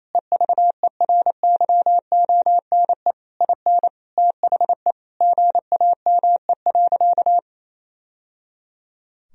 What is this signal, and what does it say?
Morse code, heard as a keyed tone, about 28 wpm, 700 Hz. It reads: EVERYONE IN THE GAME.
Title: Sentences from Top 1000 Words - Review Part 4 - 28wpm